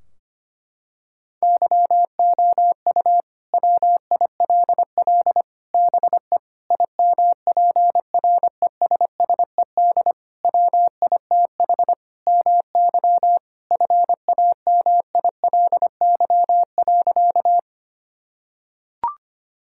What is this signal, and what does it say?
Morse code, heard as a keyed tone, about 25 wpm, 700 Hz. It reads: YOU WILL BE IMPRESSED WITH MY FAMILY.
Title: Sentences from Top 1600 Words 25wpm